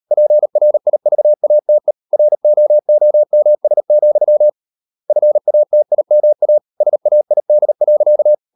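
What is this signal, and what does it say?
Morse code, heard as a keyed tone, about 38 wpm, 600 Hz. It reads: PRIVATE ROOMS, FATIMA SAID.
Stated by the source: Morse code practice at 38 WPM, sending Ken Follett's novel Never in English, from CW med Gnister.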